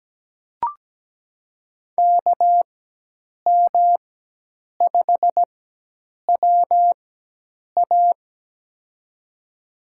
Morse code, heard as a keyed tone, about 17 wpm, spaced as for 8 wpm, 700 Hz. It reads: KM5WA
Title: USA Call Signs - 5 Characters - 4x Character Spacing - 17wpm